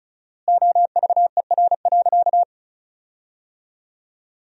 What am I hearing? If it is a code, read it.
Morse code, heard as a keyed tone, about 36 wpm, 700 Hz. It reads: OVER.
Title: 4x Word-Spacing - Sentences from Top 500 Words - 35wpm